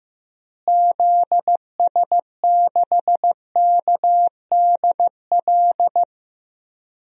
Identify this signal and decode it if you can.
Morse code, heard as a keyed tone, about 15 wpm, 700 Hz. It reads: ZS6KDL
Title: DXCC-100 Call Signs - 6 Characters - Repeated 2x - 15wpm